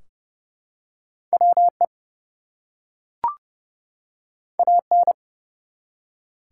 Morse code, heard as a keyed tone, about 30 wpm, 700 Hz. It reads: WE AN
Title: Two Letter Words - 30wpm